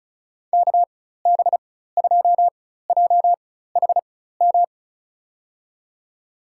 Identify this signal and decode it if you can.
Morse code, heard as a keyed tone, about 36 wpm, spaced as for 16 wpm, 700 Hz. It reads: KB2JHM
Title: USA Call Signs - 6 Characters - 4x Character Spacing - 35wpm